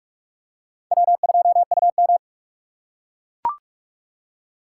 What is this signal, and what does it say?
Morse code, heard as a keyed tone, about 42 wpm, 700 Hz. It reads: W2UM